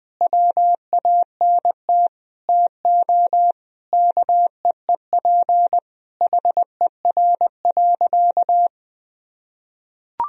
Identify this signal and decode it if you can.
Morse code, heard as a keyed tone, about 20 wpm, 700 Hz. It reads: WANT TO KEEP HER.
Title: Sentences from Top 1000 Words - Review Part 2 - 20wpm